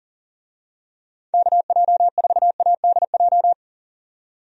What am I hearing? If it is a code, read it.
Morse code, heard as a keyed tone, about 40 wpm, 700 Hz. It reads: KJ4ADJ
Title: USA Call Signs - 6 Characters - 40wpm